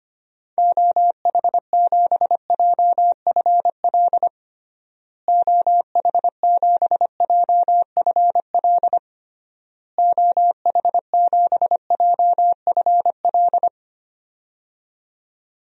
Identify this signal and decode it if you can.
Morse code, heard as a keyed tone, about 25 wpm, 700 Hz. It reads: OH7JFL OH7JFL OH7JFL
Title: DXCC-100 Call Signs - 6 Characters - Repeated 3x - 25wpm